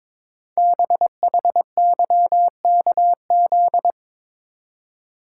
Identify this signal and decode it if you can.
Morse code, heard as a keyed tone, about 23 wpm, 700 Hz. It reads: BHYKZ
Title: Code Groups - 5 Letters - 22wpm